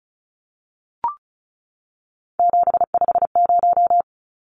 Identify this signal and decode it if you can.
Morse code, heard as a keyed tone, about 34 wpm, 700 Hz. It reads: 750
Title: Numbers - 3-Digit 35wpm